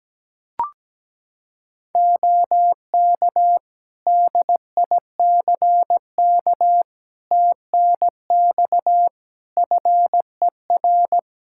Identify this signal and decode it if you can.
Morse code, heard as a keyed tone, about 17 wpm, 700 Hz. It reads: OK DICK TNX FER
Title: Practice QSO - Exchange 4 - Ending - 17wpm